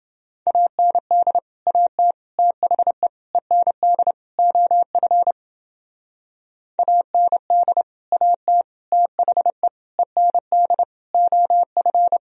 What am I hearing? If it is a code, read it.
Morse code, heard as a keyed tone, about 30 wpm, 700 Hz. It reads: AND AT THE END OF AND AT THE END OF